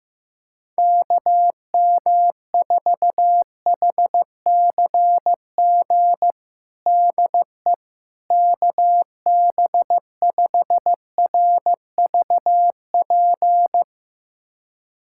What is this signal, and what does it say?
Morse code, heard as a keyed tone, about 15 wpm, 700 Hz. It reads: KM4HCG DE KB5RVP
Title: USA Callsign DE Callsign 15wpm